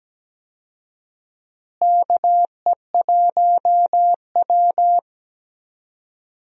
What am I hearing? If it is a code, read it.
Morse code, heard as a keyed tone, about 17 wpm, 700 Hz. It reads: KE1W
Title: USA Call Signs - 4 Characters - Repeated 3x - 17wpm